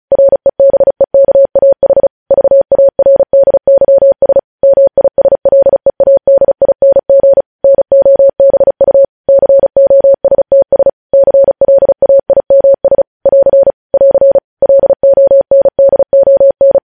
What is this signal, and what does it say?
Morse code, heard as a keyed tone, about 36 wpm, 550 Hz. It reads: REBEKAH VARDYS MISLEADING NOBU COSTS CLAIMS <AR> <AR> LONDON